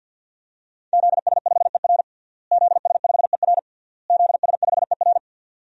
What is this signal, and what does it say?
Morse code, heard as a keyed tone, about 49 wpm, 700 Hz. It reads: ZS5ER ZS5ER ZS5ER
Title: DXCC-100 Call Signs - 5 Characters - Repeated 3x - 50wpm